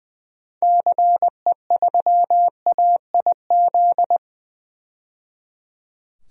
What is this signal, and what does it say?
Morse code, heard as a keyed tone, about 20 wpm, 700 Hz. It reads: CE3AIZ
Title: DXCC-100 Call Signs - 6 Characters - 20wpm